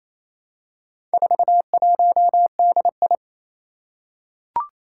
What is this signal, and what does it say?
Morse code, heard as a keyed tone, about 29 wpm, 700 Hz. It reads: 41DI
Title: Four Challenging Letter-Number Combinations 28wpm